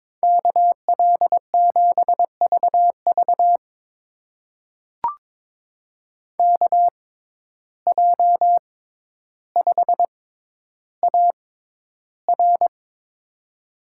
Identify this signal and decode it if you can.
Morse code, heard as a keyed tone, about 22 wpm, spaced as for 11 wpm, 700 Hz. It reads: KL7VV KJ5AR
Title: USA Call Signs - 5 Characters - 6x Character Spacing - 22wpm